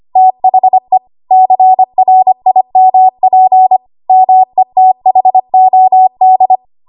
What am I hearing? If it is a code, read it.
Morse code, heard as a keyed tone, about 25 wpm, 750 Hz. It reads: THE CRIMP METHOD
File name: Archive / 25 WPM / 25WPM